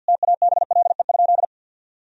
Morse code, heard as a keyed tone, about 52 wpm, 700 Hz. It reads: TABLE?
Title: Sentences from Top 1000 Words - Review Part 2 - 50wpm